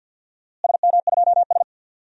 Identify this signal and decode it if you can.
Morse code, heard as a keyed tone, about 52 wpm, 700 Hz. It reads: SM2S
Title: DXCC-100 Call Signs - 4 Characters - Repeated 2x - 50wpm